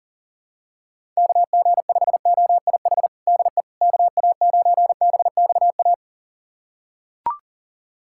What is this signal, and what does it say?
Morse code, heard as a keyed tone, about 40 wpm, 700 Hz. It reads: KG5OIH DE KA9BXA
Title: USA Callsign DE Callsign - 6 Characters - 40wpm